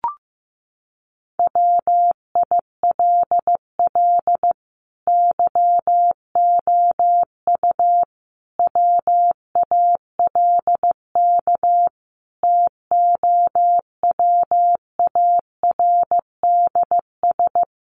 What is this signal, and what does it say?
Morse code, heard as a keyed tone, about 15 wpm, 700 Hz. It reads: WILL YOU WALK TOWARDS
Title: Sentences from Top 1100 Words 15wpm